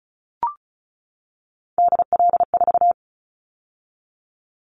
Morse code, heard as a keyed tone, about 36 wpm, 700 Hz. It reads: DL4
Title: Callsign Prefixes for Top 50 DXCC - 35wpm